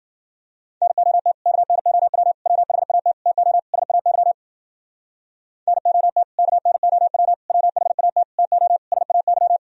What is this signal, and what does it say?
Morse code, heard as a keyed tone, about 52 wpm, 700 Hz. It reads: NOT KNOW WHAT TO SAY NOT KNOW WHAT TO SAY